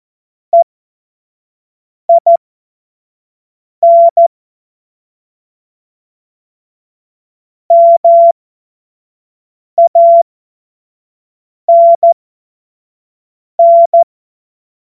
Morse code, heard as a keyed tone, about 14 wpm, spaced as for 5 wpm, 680 Hz. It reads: EIN MANN
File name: Morsekurs / Witze / Witz 18. Mai 2021, 25 F70